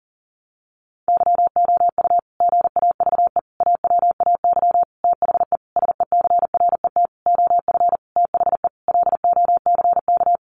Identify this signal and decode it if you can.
Morse code, heard as a keyed tone, about 37 wpm, 700 Hz. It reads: YOU GAVE AWAY THE SECRET OF THE LOCK
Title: Sentences from Top 1300 Words 40wpm